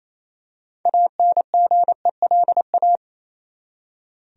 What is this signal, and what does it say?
Morse code, heard as a keyed tone, about 28 wpm, 700 Hz. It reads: ANGELA
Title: USA Common Names - 28wpm